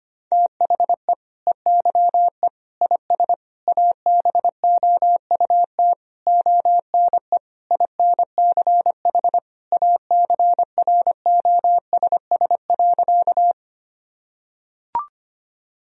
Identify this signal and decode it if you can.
Morse code, heard as a keyed tone, about 25 wpm, 700 Hz. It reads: THE EYE IS ABOUT ONE INCH ACROSS.
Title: Sentences from Top 1000 Words - Review Part 3 - 25wpm